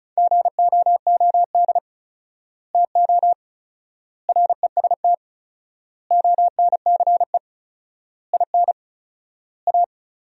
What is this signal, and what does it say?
Morse code, heard as a keyed tone, about 34 wpm, 700 Hz. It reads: GOOD TO REST ONCE IN A
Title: 4x Word-Spacing - Sentences from Top 500 Words - 35wpm